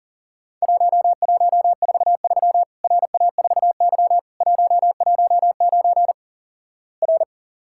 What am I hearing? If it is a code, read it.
Morse code, heard as a keyed tone, about 44 wpm, 700 Hz. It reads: 1143 RA4Y 119 R
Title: Worked All Europe DX Contest - QTC Exchange - 40wpm